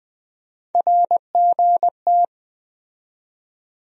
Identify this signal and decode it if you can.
Morse code, heard as a keyed tone, about 20 wpm, 700 Hz. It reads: RGT
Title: USA Call Sign Postfixes - 3 Letters 20wpm